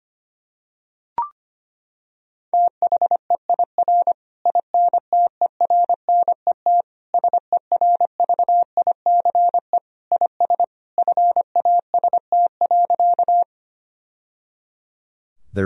Morse code, heard as a keyed tone, about 25 wpm, 700 Hz. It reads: THEIR INTERNET SERVICE IS FAST.
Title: Sentences from Top 1000 Words - Review Part 4 - 25wpm